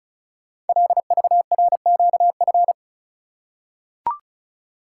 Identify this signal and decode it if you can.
Morse code, heard as a keyed tone, about 36 wpm, 700 Hz. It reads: LVRQF
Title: Code Groups - 5 Letters - 35wpm